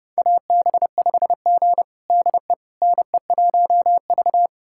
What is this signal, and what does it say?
Morse code, heard as a keyed tone, about 32 wpm, 700 Hz. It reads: AB5G DE NE1V
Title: USA Callsign DE Callsign - 4 Characters - 30wpm